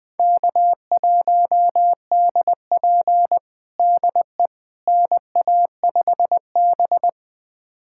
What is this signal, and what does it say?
Morse code, heard as a keyed tone, about 20 wpm, 700 Hz. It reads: K1DP DE NA5B
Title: USA Callsign DE Callsign - 4 Characters - 20wpm